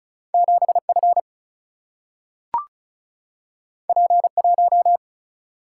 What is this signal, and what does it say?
Morse code, heard as a keyed tone, about 36 wpm, 700 Hz. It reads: 7F P1
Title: Single-Letter Single-Number 35wpm